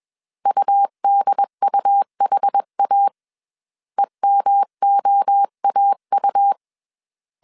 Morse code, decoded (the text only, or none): UDUHA EMOAU